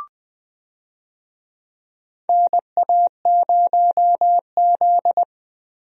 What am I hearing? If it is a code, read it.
Morse code, heard as a keyed tone, about 20 wpm, 700 Hz. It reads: NA0Z
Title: USA Call Signs - 4 Characters - 20wpm